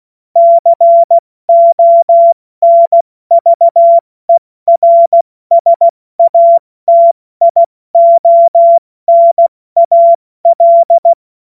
Morse code, heard as a keyed tone, about 16 wpm, 680 Hz. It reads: CONVERSATIONAL
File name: Morsekurs / Witze / Witz 23 Januar 2024 70 F80